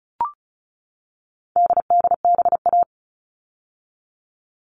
Morse code, heard as a keyed tone, about 34 wpm, 700 Hz. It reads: DDBA
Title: All Cut Numbers - 4-Digit 35wpm